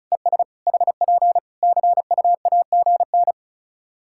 Morse code, heard as a keyed tone, about 36 wpm, 700 Hz. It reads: ES HP CUAGN
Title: Practice QSO - Exchange 4 - Ending - 35wpm